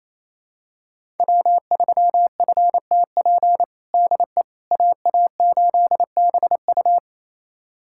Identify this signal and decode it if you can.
Morse code, heard as a keyed tone, about 28 wpm, 700 Hz. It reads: W3FTP DE AA8BU
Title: USA Callsign DE Callsign - 5 Characters - 28wpm